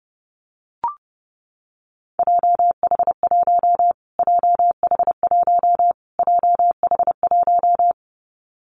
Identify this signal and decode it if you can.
Morse code, heard as a keyed tone, about 30 wpm, 700 Hz. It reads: JH1 JH1 JH1
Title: Callsign Prefixes for Top 25 DXCC - Repeated 3x - 30wpm